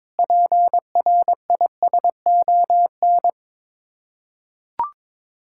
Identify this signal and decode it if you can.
Morse code, heard as a keyed tone, about 21 wpm, 700 Hz. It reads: PRISON
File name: practice / 22wpm / Sets of 6 Words - 22wpm